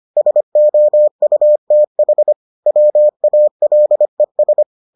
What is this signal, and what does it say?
Morse code, heard as a keyed tone, about 25 wpm, 600 Hz. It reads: SOUTH WALES